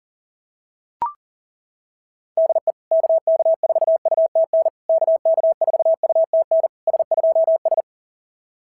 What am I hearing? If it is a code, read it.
Morse code, heard as a keyed tone, about 40 wpm, 690 Hz. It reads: DE KK4UTN KK4UTN S2S
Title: SOTA QSO Practice 40wpm